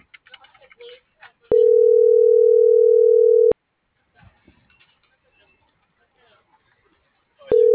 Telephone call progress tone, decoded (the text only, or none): ringback tone